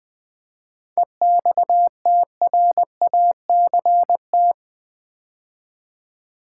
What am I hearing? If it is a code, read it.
Morse code, heard as a keyed tone, about 20 wpm, 700 Hz. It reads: EXTRACT
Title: 7 Letter Words - First 3 Letters Given - 20wpm